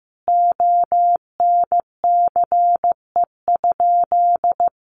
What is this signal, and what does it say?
Morse code, heard as a keyed tone, about 15 wpm, 700 Hz. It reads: ONCE?
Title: Sentences from Top 1100 Words 15wpm